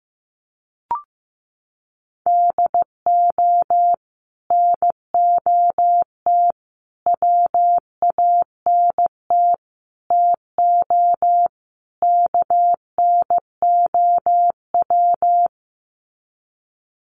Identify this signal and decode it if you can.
Morse code, heard as a keyed tone, about 14 wpm, 700 Hz. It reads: DO NOT WANT TO KNOW
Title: Sets of 5 Words Encore 15wpm